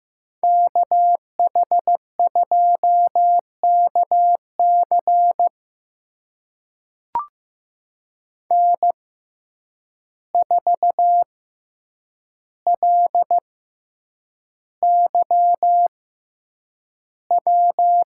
Morse code, heard as a keyed tone, about 15 wpm, spaced as for 8 wpm, 700 Hz. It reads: KH2KC N4LYW
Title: USA Call Signs - 5 Characters - 6x Character Spacing - 15wpm